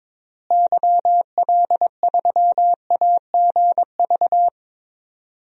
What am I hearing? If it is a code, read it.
Morse code, heard as a keyed tone, about 22 wpm, 700 Hz. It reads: YL3AGV
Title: DXCC-100 Call Signs - 6 Characters - Repeated 2x - 22wpm